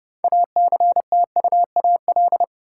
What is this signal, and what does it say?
Morse code, heard as a keyed tone, about 30 wpm, 700 Hz. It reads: ACTUAL
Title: Sentences from Top 1000 Words - Review Part 5 - New Word Drill - 30wpm